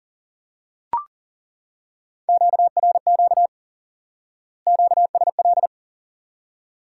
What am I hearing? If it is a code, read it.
Morse code, heard as a keyed tone, about 40 wpm, 700 Hz. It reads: QRQ QSL
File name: IQE Course 3 - Lesson 89 - YAGI - Two Elements - 40wpm